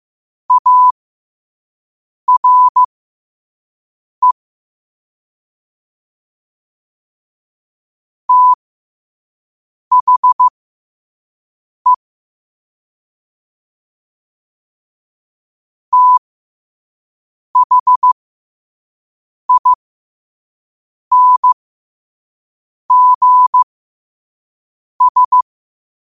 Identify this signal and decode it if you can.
Morse code, heard as a keyed tone, about 15 wpm, spaced as for 5 wpm, 1000 Hz. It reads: ARE THE THINGS
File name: 5wpm